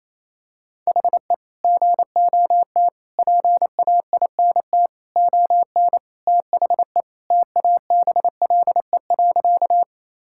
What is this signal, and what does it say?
Morse code, heard as a keyed tone, about 28 wpm, 700 Hz. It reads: HE GOT PAINT ON THE TABLE.
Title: Sentences from Top 1000 Words - Review Part 6 - 28wpm